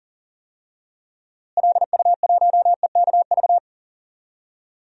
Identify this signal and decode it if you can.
Morse code, heard as a keyed tone, about 40 wpm, 700 Hz. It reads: LU1EKV
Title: DXCC-100 Call Signs - 6 Characters - Repeated 3x - 40wpm